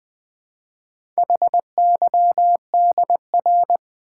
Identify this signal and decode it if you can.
Morse code, heard as a keyed tone, about 20 wpm, 700 Hz. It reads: HYDR
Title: Build A Word - 2 Letters At A Time - 8-Letter Words - 20wpm